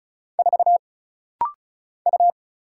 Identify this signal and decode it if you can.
Morse code, heard as a keyed tone, about 36 wpm, 700 Hz. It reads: 4 U
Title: Single Letter-Number Alt - Rapid-Fire 35wpm